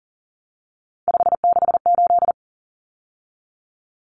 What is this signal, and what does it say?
Morse code, heard as a keyed tone, about 40 wpm, 700 Hz. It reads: 568
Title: Numbers - 3-Digit 40wpm